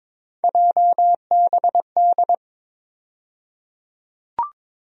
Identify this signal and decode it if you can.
Morse code, heard as a keyed tone, about 22 wpm, 700 Hz. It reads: JBD
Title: USA Call Sign Postfixes - 3 Letters 22wpm